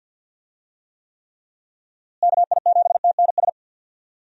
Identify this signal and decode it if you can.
Morse code, heard as a keyed tone, about 49 wpm, 700 Hz. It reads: KI7TNS
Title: USA Call Signs - 6 Characters - Repeated 3x - 50wpm